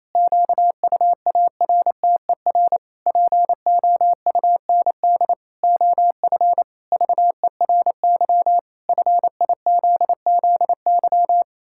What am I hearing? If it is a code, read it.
Morse code, heard as a keyed tone, about 28 wpm, 700 Hz. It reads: QUARTER POUND OF VERY FIZZY
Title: Pangram Challenge v2 - 28wpm